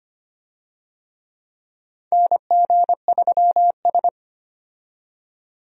Morse code, heard as a keyed tone, about 25 wpm, 700 Hz. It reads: NG3S
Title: USA Call Signs - 4 Characters - 25wpm